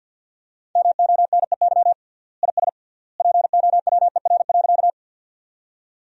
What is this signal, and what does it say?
Morse code, heard as a keyed tone, about 47 wpm, 700 Hz. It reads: MONEY IS POWER.
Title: 3x Word-Spacing - Sentences from Top 500 Words - 50wpm